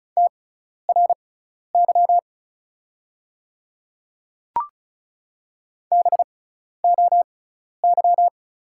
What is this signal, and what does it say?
Morse code, heard as a keyed tone, about 36 wpm, spaced as for 12 wpm, 690 Hz. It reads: TRY BOY
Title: 6x Character Spacing - Top 200 Words - 35wpm